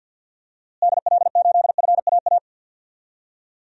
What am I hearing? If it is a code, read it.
Morse code, heard as a keyed tone, about 52 wpm, 700 Hz. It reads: DL8FAA
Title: DXCC-100 Call Signs - 6 Characters - Repeated 2x - 50wpm